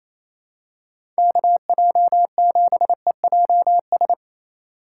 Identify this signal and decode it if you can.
Morse code, heard as a keyed tone, about 28 wpm, 700 Hz. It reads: KJ7EJS
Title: USA Call Signs - 8x Character Spacing - 28wpm